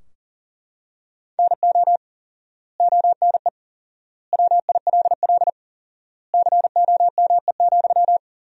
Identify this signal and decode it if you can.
Morse code, heard as a keyed tone, about 40 wpm, 700 Hz. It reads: NO ONE WILL COME,
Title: 4x Word-Spacing - Sentences from Top 100 Words - 40wpm